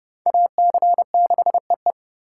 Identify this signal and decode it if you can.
Morse code, heard as a keyed tone, about 30 wpm, 700 Hz. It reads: AC6EE
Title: USA Call Signs - 5 Characters - 3x Character Spacing - 30wpm